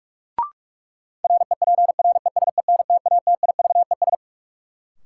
Morse code, heard as a keyed tone, about 46 wpm, 700 Hz. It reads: REPRESENTATIVES
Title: Sets of 5 Words Encore - New Word Drill - 45wpm